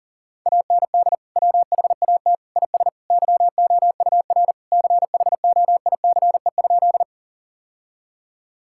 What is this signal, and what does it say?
Morse code, heard as a keyed tone, about 40 wpm, 700 Hz. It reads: AND WHAT IS YOUR CHOICE?
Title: Sentences from Top 1000 Words - Review Part 6 - 40wpm